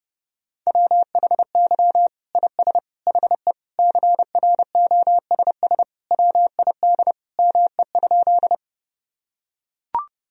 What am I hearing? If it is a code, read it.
Morse code, heard as a keyed tone, about 32 wpm, 700 Hz. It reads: WHY IS HE CROSS WID ME?
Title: Sentences from Top 300 Words and Abbreviations 30wpm